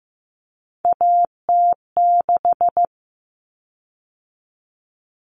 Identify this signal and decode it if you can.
Morse code, heard as a keyed tone, about 15 wpm, 700 Hz. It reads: AT6
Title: Common Cut Numbers - 3-Digit 15wpm